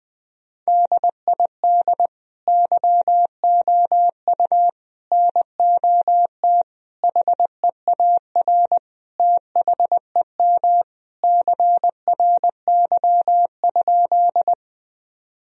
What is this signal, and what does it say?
Morse code, heard as a keyed tone, about 20 wpm, 700 Hz. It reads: DID YOU NOT HEAR THEM CRY?
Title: Sentences from Top 1000 Words - Review Part 2 - 20wpm